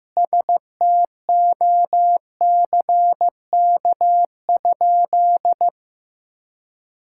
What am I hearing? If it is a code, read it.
Morse code, heard as a keyed tone, about 15 wpm, 700 Hz. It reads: STOCK?
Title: Sentences from Top 1000 Words - Review Part 5 - 15wpm